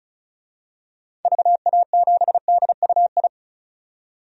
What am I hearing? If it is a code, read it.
Morse code, heard as a keyed tone, about 36 wpm, 700 Hz. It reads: VA7DUI
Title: DXCC-100 Call Signs - 6 Characters - 35wpm